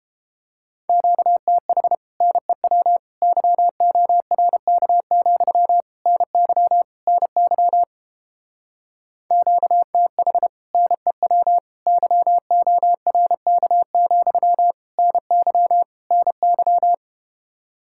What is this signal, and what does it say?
Morse code, heard as a keyed tone, about 29 wpm, 700 Hz. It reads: QTH NEW YORK, NY NY QTH NEW YORK, NY NY